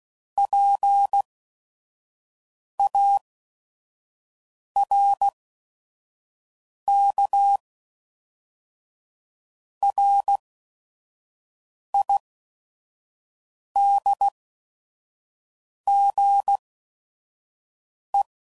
Morse code, heard as a keyed tone, about 16 wpm, spaced as for 5 wpm, 800 Hz. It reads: PARKRIDGE
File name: CW Code Groups @5WPM / fromSM's / Cities